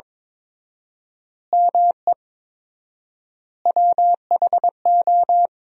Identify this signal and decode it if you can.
Morse code, heard as a keyed tone, about 21 wpm, 700 Hz. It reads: ME WHO
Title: 4x Word-Spacing - Sentences from Top 200 Words - 22wpm